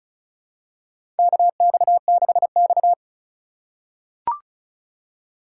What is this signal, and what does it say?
Morse code, heard as a keyed tone, about 34 wpm, 700 Hz. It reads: KX6X
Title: K1USN SST QSO Practice - 35wpm